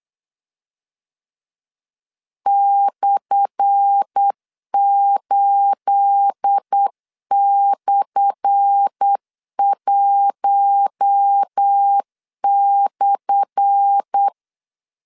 Morse code, heard as a keyed tone, about 8 wpm, 800 Hz. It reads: /8/1/
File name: LEZIONI / 10 SIGNS / 10A SLASH 8wpm 3452